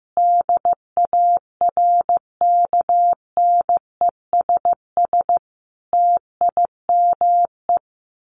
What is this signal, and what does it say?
Morse code, heard as a keyed tone, about 15 wpm, 700 Hz. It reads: DARKNESS TIME
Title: A Sound Of Thunder 15wpm